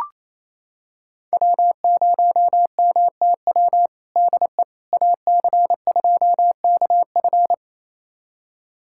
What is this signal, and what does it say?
Morse code, heard as a keyed tone, about 28 wpm, 700 Hz. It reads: W0MTW DE AC2KF
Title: USA Callsign DE Callsign - 5 Characters - 28wpm